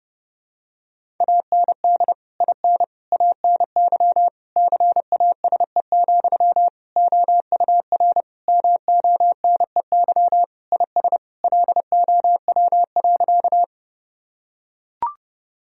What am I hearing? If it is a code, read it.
Morse code, heard as a keyed tone, about 30 wpm, 700 Hz. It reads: AND IN ANY CASE, OUR MONEY IS LOW.
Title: Sentences from Top 1000 Words - Review Part 1 - 30wpm